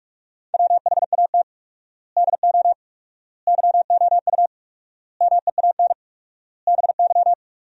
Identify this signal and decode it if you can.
Morse code, heard as a keyed tone, about 42 wpm, 700 Hz. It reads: WHAT DO YOU MEAN BY
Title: 4x Word-Spacing - Sentences from Top 400 Words - 45wpm